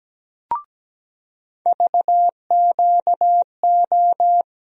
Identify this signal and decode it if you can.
Morse code, heard as a keyed tone, about 17 wpm, 700 Hz. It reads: VQO
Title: USA Call Sign Postfixes - 3 Letters 17wpm